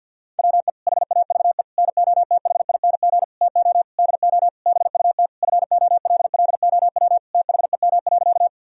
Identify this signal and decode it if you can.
Morse code, heard as a keyed tone, about 50 wpm, 700 Hz. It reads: WE HAVE NOTHING TO DO BUT FOLLOW THEM.
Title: Sentences from Top 1000 Words - Review Part 1 - 50wpm